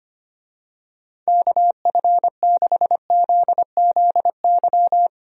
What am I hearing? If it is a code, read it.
Morse code, heard as a keyed tone, about 25 wpm, 700 Hz. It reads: KF6ZZY
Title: USA Call Signs - 6 Characters - Repeated 3x - 25wpm